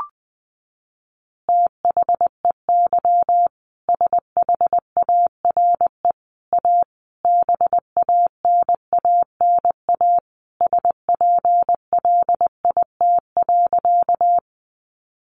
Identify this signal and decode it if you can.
Morse code, heard as a keyed tone, about 20 wpm, 700 Hz. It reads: THEY SHARE A BANANA SPLIT.
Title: Easy Phrases 20wpm